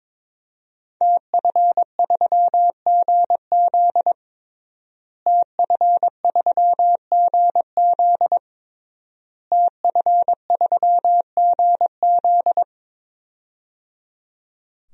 Morse code, heard as a keyed tone, about 22 wpm, 700 Hz. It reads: TF3GZ TF3GZ TF3GZ
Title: DXCC-100 Call Signs - 5 Characters - Repeated 3x - 22wpm